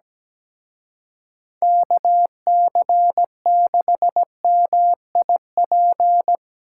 Morse code, heard as a keyed tone, about 20 wpm, 700 Hz. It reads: KC6MIP
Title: Speed Racing - USA 2x3 Call Signs 17wpm